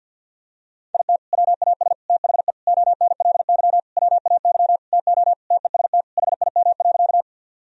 Morse code, heard as a keyed tone, about 50 wpm, 700 Hz. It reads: IT WAS THE ONLY WAY TO TEST HIM.
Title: Sentences from Top 1000 Words - Review Part 3 - 50wpm